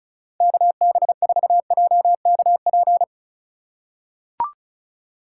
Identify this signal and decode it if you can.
Morse code, heard as a keyed tone, about 36 wpm, 700 Hz. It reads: KB4JKP
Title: USA Call Signs - 6 Characters - 3x Character Spacing - 35wpm